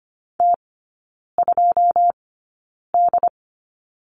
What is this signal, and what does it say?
Morse code, heard as a keyed tone, about 25 wpm, spaced as for 9 wpm, 700 Hz. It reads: T2D
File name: course / 25:10wpm / 093 Lesson 16e - Letter c - Three Characters 25wpm